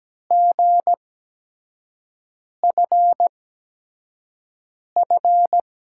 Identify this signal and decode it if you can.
Morse code, heard as a keyed tone, about 17 wpm, spaced as for 5 wpm, 700 Hz. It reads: GFF
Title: USA Call Signs - 6 Characters - 8x Character Spacing - 17wpm